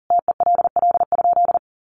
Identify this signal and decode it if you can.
Morse code, heard as a keyed tone, about 40 wpm, 700 Hz. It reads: TELL?
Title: Sentences from Top 300 Words 40wpm